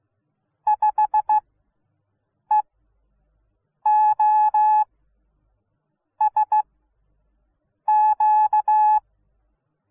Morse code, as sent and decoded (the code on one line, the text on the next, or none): ..... --- ... --.-
5EOSQ